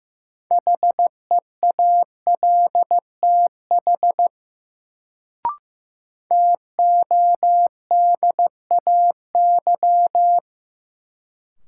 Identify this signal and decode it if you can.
Morse code, heard as a keyed tone, about 15 wpm, 700 Hz. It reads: HEALTH TODAY
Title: Sentences from Top 600 Words - New Word Drill - 15wpm